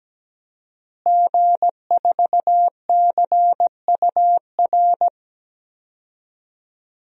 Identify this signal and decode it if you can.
Morse code, heard as a keyed tone, about 17 wpm, 700 Hz. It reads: G4CUR
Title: DXCC-100 Call Signs - 5 Characters - Repeated 2x - 17wpm